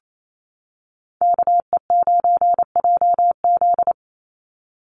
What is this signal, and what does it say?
Morse code, heard as a keyed tone, about 28 wpm, 700 Hz. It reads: KE9JZ
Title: USA Call Signs 28wpm